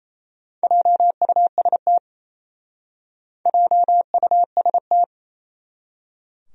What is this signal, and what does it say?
Morse code, heard as a keyed tone, about 35 wpm, 700 Hz. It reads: JUST JUST